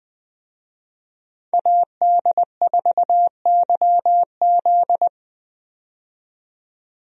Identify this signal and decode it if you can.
Morse code, heard as a keyed tone, about 20 wpm, 700 Hz. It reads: AD4YZ
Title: USA Call Signs - 5 Characters - 20wpm